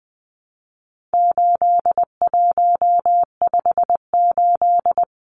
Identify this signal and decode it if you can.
Morse code, heard as a keyed tone, about 20 wpm, 700 Hz. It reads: 8158
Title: Numbers - 4-Digit 20wpm